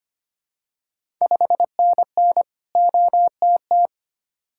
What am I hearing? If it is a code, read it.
Morse code, heard as a keyed tone, about 25 wpm, 700 Hz. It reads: 5NN OTT
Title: Ontario QSO Party Exchange 25wpm